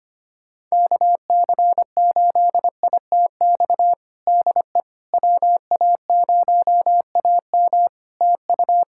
Morse code, heard as a keyed tone, about 25 wpm, 700 Hz. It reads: KC8ITX DE WA0AM TU
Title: Practice QSO - Exchange 4 - Ending - 25wpm